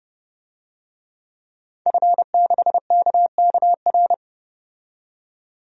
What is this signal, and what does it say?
Morse code, heard as a keyed tone, about 31 wpm, 700 Hz. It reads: F6KKR
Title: DXCC-100 Call Signs - 5 Characters - 30wpm